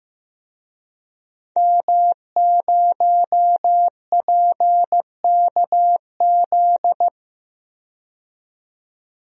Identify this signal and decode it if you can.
Morse code, heard as a keyed tone, about 15 wpm, 700 Hz. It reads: M0PKZ
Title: DXCC-100 Call Signs - 5 Characters - 15wpm